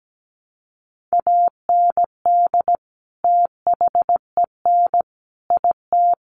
Morse code, heard as a keyed tone, about 17 wpm, 700 Hz. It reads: AND THEN IT
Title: Sentences from Top 1200 Words 17wpm